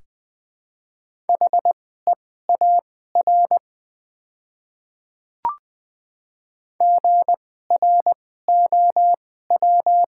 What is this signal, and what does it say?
Morse code, heard as a keyed tone, about 20 wpm, spaced as for 14 wpm, 700 Hz. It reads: HEAR GROW
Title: 2x Character Spacing - Top 300 Words - 20wpm